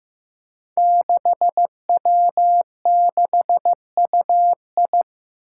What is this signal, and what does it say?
Morse code, heard as a keyed tone, about 15 wpm, 700 Hz. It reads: 6W6UI
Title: Five Challenging Letter-Number Combinations 15wpm